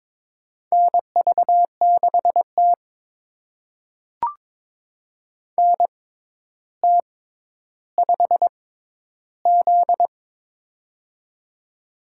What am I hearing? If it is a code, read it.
Morse code, heard as a keyed tone, about 22 wpm, spaced as for 11 wpm, 700 Hz. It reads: NV6T NT5Z
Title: USA Call Signs - 4 Characters - 6x Character Spacing - 22wpm